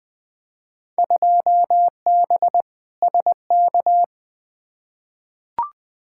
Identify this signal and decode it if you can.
Morse code, heard as a keyed tone, about 20 wpm, 700 Hz. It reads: 2B SK
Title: Field Day Exchanges Encore - 20wpm